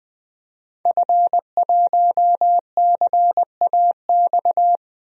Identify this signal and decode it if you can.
Morse code, heard as a keyed tone, about 20 wpm, 700 Hz. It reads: F1CAX
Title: DXCC-100 Call Signs - 5 Characters - Repeated 2x - 20wpm